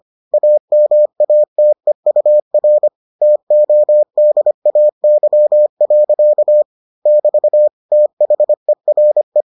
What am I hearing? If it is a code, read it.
Morse code, heard as a keyed tone, about 25 wpm, 600 Hz. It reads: AMATEUR TODAY. <BT> THERE